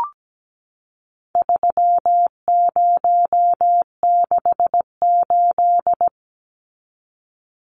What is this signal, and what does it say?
Morse code, heard as a keyed tone, about 17 wpm, 700 Hz. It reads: 3068
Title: Numbers - 4-Digit 17wpm